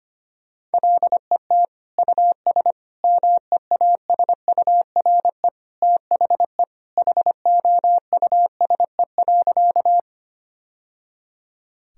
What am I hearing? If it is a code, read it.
Morse code, heard as a keyed tone, about 25 wpm, 700 Hz. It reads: LET US MEASURE THE HOUSE.
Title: Sentences from Top 1000 Words - Review Part 2 - 25wpm